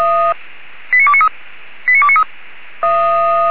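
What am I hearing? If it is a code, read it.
Five-tone selective call (ZVEI), heard as quick runs of five tones, tones of about 70 ms. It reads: 89293
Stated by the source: ZVEI five-tone selective call received by SDR.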